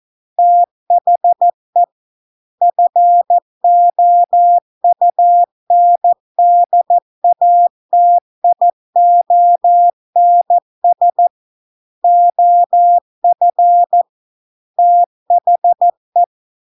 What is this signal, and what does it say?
Morse code, heard as a keyed tone, about 14 wpm, 700 Hz. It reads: THE FOUNDATIONS OF THE